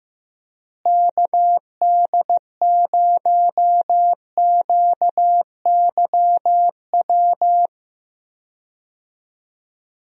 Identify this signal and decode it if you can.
Morse code, heard as a keyed tone, about 15 wpm, 700 Hz. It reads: KD0QYW
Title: USA Call Signs - 6 Characters - Repeated 2x - 15wpm